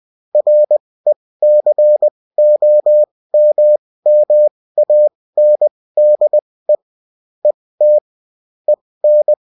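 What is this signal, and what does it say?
Morse code, heard as a keyed tone, about 20 wpm, spaced as for 16 wpm, 600 Hz. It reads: RECOMMANDE ET EN
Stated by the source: Bulletin in text and CW form.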